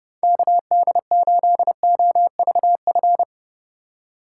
Morse code, heard as a keyed tone, about 30 wpm, 700 Hz. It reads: KD8OVF